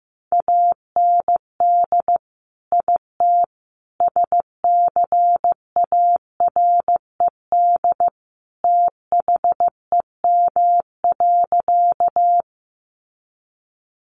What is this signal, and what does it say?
Morse code, heard as a keyed tone, about 15 wpm, 700 Hz. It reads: AND IT SCARED THEM.
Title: Sentences from Top 700 Words 15wpm